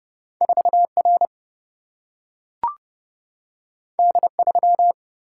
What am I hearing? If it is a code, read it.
Morse code, heard as a keyed tone, about 30 wpm, 700 Hz. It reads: 4R D3
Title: Single-Letter Single-Number 30wpm